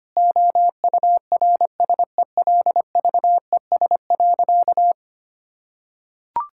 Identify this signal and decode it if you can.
Morse code, heard as a keyed tone, about 25 wpm, 700 Hz. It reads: OURSELVES.
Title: Sentences from Top 1600 Words 25wpm